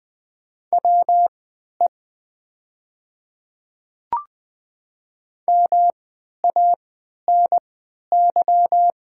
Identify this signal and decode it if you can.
Morse code, heard as a keyed tone, about 20 wpm, spaced as for 11 wpm, 700 Hz. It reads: WE MANY
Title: 3x Character Spacing - Top 100 Words - 20wpm